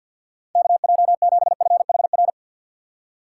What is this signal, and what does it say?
Morse code, heard as a keyed tone, about 53 wpm, 700 Hz. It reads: KJ7FHR
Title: USA Call Signs - 6 Characters - 4x Character Spacing - 50wpm